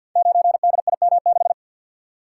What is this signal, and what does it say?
Morse code, heard as a keyed tone, about 46 wpm, 700 Hz. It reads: 9DIM6
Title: Code Groups - 5 Letter:Numbers - 50wpm